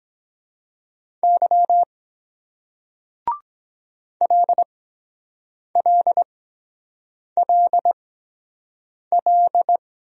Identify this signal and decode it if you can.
Morse code, heard as a keyed tone, about 23 wpm, spaced as for 7 wpm, 700 Hz. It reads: Y LLLL